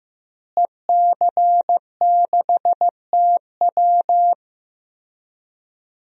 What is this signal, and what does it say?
Morse code, heard as a keyed tone, about 15 wpm, 700 Hz. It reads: EC6TW
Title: DXCC-100 Call Signs - 5 Characters - Repeated 3x - 15wpm